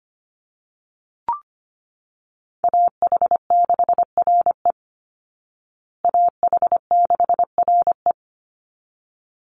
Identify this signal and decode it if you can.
Morse code, heard as a keyed tone, about 25 wpm, 700 Hz. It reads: AH6RE AH6RE
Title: Call Signs Repeated 3x - 25wpm